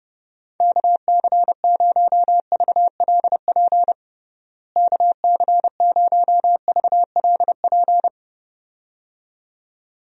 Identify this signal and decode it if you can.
Morse code, heard as a keyed tone, about 30 wpm, 700 Hz. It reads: KC0VLP KC0VLP
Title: USA Call Signs - 6 Characters - Repeated 2x - 30wpm